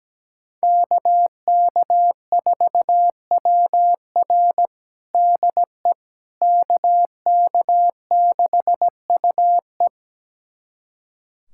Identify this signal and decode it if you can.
Morse code, heard as a keyed tone, about 17 wpm, 700 Hz. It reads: KK4WR DE KK6UE
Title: USA Callsign DE Callsign - 5 Characters - 17wpm